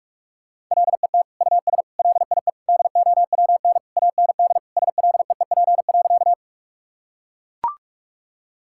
Morse code, heard as a keyed tone, about 41 wpm, 700 Hz. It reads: LET US LIE DOWN AND SLEEP.
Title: Sentences from Top 1600 Words 45wpm